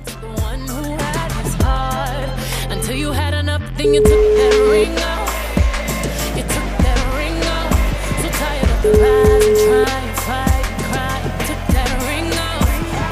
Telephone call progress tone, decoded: ringback tone

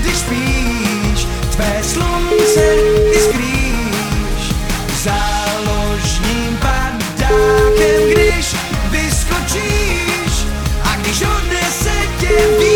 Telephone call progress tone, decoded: ringback tone

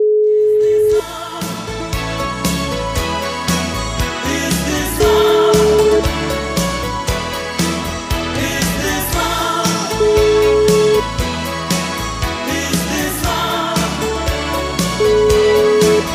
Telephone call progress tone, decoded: ringback tone

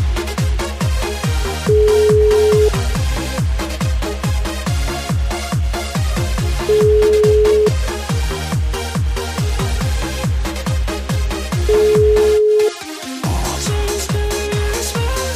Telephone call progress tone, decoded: ringback tone

